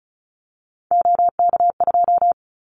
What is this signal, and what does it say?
Morse code, heard as a keyed tone, about 34 wpm, 700 Hz. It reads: OK2
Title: Callsign Prefixes for Top 100 DXCC - 35wpm